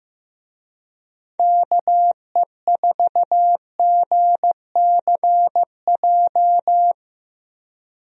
Morse code, heard as a keyed tone, about 15 wpm, 700 Hz. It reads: KE4GCJ
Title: USA Call Signs - 6 Characters - Repeated 3x - 15wpm